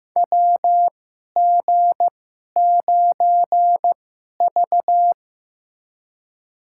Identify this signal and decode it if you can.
Morse code, heard as a keyed tone, about 15 wpm, spaced as for 11 wpm, 700 Hz. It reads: WG9V
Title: USA Call Signs - 4 Characters - 2x Character Spacing - 15wpm